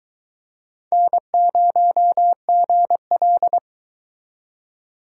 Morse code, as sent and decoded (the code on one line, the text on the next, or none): -. ----- --. .-..
N0GL